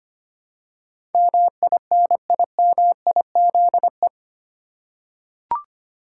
Morse code, heard as a keyed tone, about 25 wpm, 700 Hz. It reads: MINIMIZE
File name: 8 Letter Words - First 3 Letters Given - 25wpm